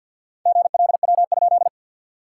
Morse code, heard as a keyed tone, about 48 wpm, 700 Hz. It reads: GLW?
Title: SOTA QSO Practice 50wpm